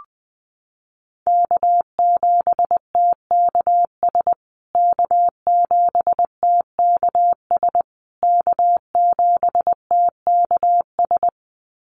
Morse code, heard as a keyed tone, about 19 wpm, 700 Hz. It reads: K7TKS K7TKS K7TKS
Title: USA Call Signs Repeated 3x 20wpm